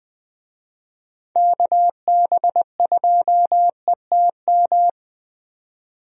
Morse code, heard as a keyed tone, about 20 wpm, 700 Hz. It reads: KB2ETM
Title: USA Call Signs - 6 Characters - 20wpm